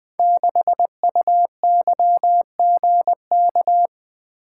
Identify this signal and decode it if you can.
Morse code, heard as a keyed tone, about 20 wpm, 700 Hz. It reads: 6UYGK
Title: Code Groups - 5 Letter:Numbers - 20wpm